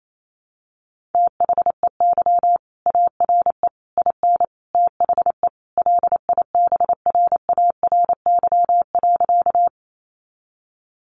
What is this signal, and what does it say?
Morse code, heard as a keyed tone, about 28 wpm, 700 Hz. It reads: THEY ARE IN THE LIBRARY.
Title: Sentences from Top 600 Words 28wpm